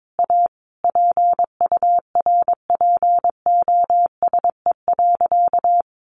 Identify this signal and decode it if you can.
Morse code, heard as a keyed tone, about 22 wpm, 700 Hz. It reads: A PURPOSE.
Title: Sentences from Top 800 Words 22wpm